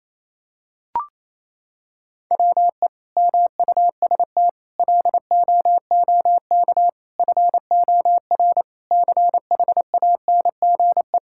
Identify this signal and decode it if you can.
Morse code, heard as a keyed tone, about 28 wpm, 700 Hz. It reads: WE MUST LOOK FOR CHANGE
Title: Sentences from Top 1600 Words 28wpm